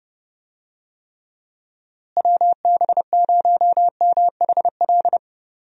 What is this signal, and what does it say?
Morse code, heard as a keyed tone, about 32 wpm, 700 Hz. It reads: WB0MHL
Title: USA Call Signs - 6 Characters - Repeated 2x - 30wpm